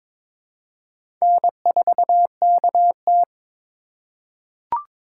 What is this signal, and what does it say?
Morse code, heard as a keyed tone, about 22 wpm, 700 Hz. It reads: N4KT
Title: USA Call Signs - 4 Characters - 4x Character Spacing - 22wpm